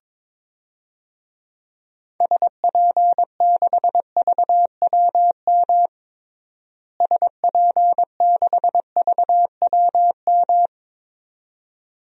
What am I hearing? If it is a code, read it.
Morse code, heard as a keyed tone, about 22 wpm, 700 Hz. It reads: SP6VWM SP6VWM
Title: DXCC-100 Call Signs - 6 Characters - Repeated 2x - 22wpm